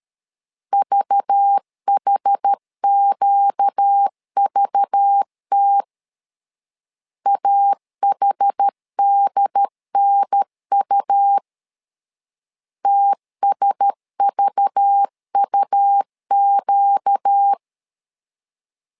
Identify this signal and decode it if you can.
Morse code, heard as a keyed tone, about 13 wpm, 800 Hz. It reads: VHQVT AHDNU TSVUQ
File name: LEZIONI / 5 LBVC / 5 LBV 12wpm 1855